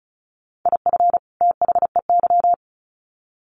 Morse code, heard as a keyed tone, about 34 wpm, 700 Hz. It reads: IF THEY
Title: Sets of 2 Words 35wpm